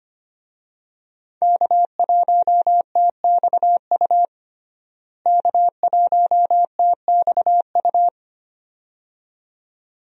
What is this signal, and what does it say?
Morse code, heard as a keyed tone, about 25 wpm, 700 Hz. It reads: K1TXU K1TXU
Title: USA Call Signs - 5 Characters - Repeated 2x - 25wpm